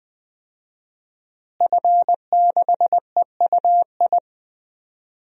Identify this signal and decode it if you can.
Morse code, heard as a keyed tone, about 20 wpm, 700 Hz. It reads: F6EUI
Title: DXCC-100 Call Signs - 5 Characters - Repeated 2x - 20wpm